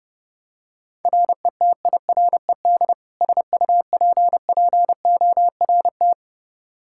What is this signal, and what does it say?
Morse code, heard as a keyed tone, about 32 wpm, 700 Hz. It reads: RETIRED SUPPORT
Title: Practice QSO - Exchange 3 - More Info - 30wpm